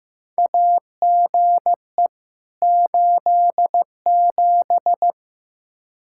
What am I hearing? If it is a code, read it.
Morse code, heard as a keyed tone, about 15 wpm, 700 Hz. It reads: AGE 87
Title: K1USN SST QSO Practice 15wpm